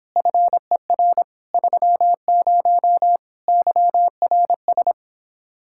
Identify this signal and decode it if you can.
Morse code, heard as a keyed tone, about 26 wpm, 700 Hz. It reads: FER 30 YRS